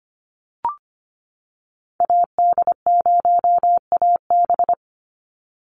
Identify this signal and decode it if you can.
Morse code, heard as a keyed tone, about 25 wpm, 700 Hz. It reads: AD0AB
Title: Call Signs - 25wpm